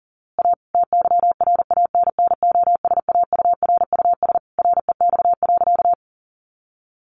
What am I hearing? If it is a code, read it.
Morse code, heard as a keyed tone, about 37 wpm, 700 Hz. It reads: A TYRANNOSAURUS REX.
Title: A Sound Of Thunder 40wpm